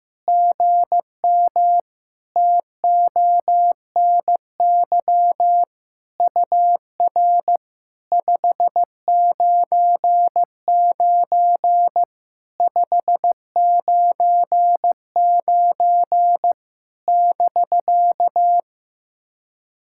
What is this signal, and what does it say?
Morse code, heard as a keyed tone, about 15 wpm, 700 Hz. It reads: GM TONY UR 599 599 <BK>
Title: SOTA QSO Practice 15wpm